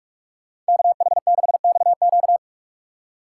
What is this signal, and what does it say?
Morse code, heard as a keyed tone, about 61 wpm, spaced as for 53 wpm, 700 Hz. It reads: KH6XQ
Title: Speed Racing - USA Call Signs 45wpm